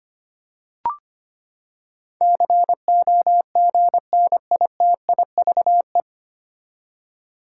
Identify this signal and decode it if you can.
Morse code, heard as a keyed tone, about 25 wpm, 700 Hz. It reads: COGNITIVE
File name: Nine Letter Words - 25wpm